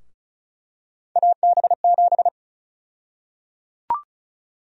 Morse code, heard as a keyed tone, about 34 wpm, 700 Hz. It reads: AB7
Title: USA Call Sign Prefixes - 35wpm